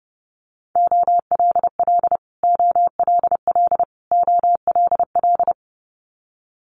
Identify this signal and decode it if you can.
Morse code, heard as a keyed tone, about 30 wpm, 700 Hz. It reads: OLL OLL OLL
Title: Multi-letter Phonemes Repeated 3x 30wpm